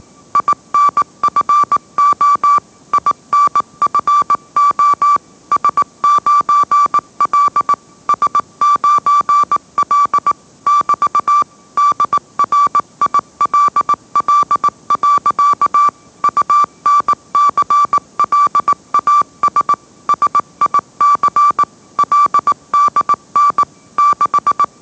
Morse code, heard as a keyed tone, about 18 wpm, 1160 Hz. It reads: INFO INFO S9L S9L <BT> DRILL. UNCLAS SIC LDN 6
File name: mrx / mrx082 20wpm